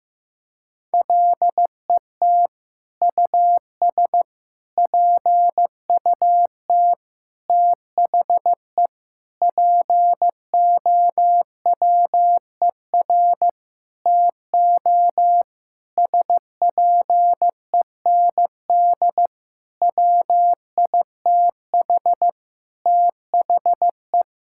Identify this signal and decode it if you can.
Morse code, heard as a keyed tone, about 15 wpm, 700 Hz. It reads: LET US PUT THE POWER TO SPEND WITH THE
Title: Sentences from Top 1000 Words - Review Part 5 - 15wpm